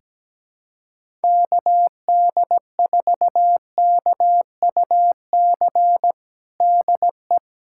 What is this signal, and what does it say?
Morse code, heard as a keyed tone, about 17 wpm, 700 Hz. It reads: KD4KUC DE
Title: USA Callsign DE Callsign 17wpm